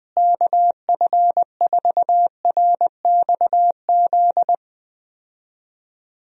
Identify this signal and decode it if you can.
Morse code, heard as a keyed tone, about 19 wpm, 700 Hz. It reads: KF4RXZ
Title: USA Call Signs - 6 Characters - Repeated 3x - 20wpm